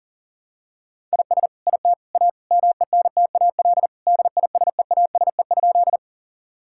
Morse code, heard as a keyed tone, about 40 wpm, 700 Hz. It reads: IS IT A MENTAL DISEASE?
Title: Sentences from Top 1000 Words - Review Part 6 - 40wpm